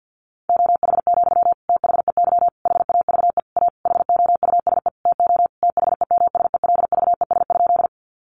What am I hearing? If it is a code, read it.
Morse code, heard as a keyed tone, about 50 wpm, 700 Hz. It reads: OH, THEY HAVE A HOUSE TO THEMSELVES?
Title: Sentences from Top 800 Words 50wpm